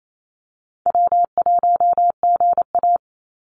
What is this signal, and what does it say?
Morse code, heard as a keyed tone, about 28 wpm, 700 Hz. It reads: W1GA
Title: USA Call Signs 28wpm